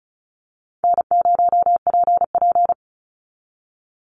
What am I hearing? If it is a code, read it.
Morse code, heard as a keyed tone, about 34 wpm, 700 Hz. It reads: N0PP
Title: Easy Call Signs Repeated 3x - 35wpm